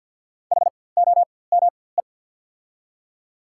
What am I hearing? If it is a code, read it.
Morse code, heard as a keyed tone, about 48 wpm, spaced as for 23 wpm, 700 Hz. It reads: HOME
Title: 4x Character Spacing - Top 300 Words - 50wpm